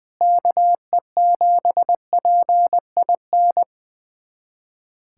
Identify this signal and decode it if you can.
Morse code, heard as a keyed tone, about 20 wpm, 700 Hz. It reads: KE7PIN
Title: USA Callsign DE Callsign - 6 Characters - 20wpm